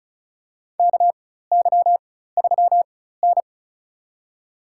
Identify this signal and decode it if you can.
Morse code, heard as a keyed tone, about 34 wpm, spaced as for 16 wpm, 700 Hz. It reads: KY3N
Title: USA Call Signs - 4 Characters - 4x Character Spacing - 35wpm